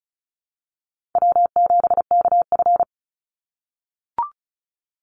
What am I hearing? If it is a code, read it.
Morse code, heard as a keyed tone, about 36 wpm, 700 Hz. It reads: W7KF
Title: Call Signs - 35wpm